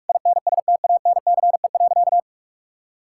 Morse code, heard as a keyed tone, about 46 wpm, 690 Hz. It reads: INSTANCE.
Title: Sentences from Top 1000 Words - Review Part 5 - 45wpm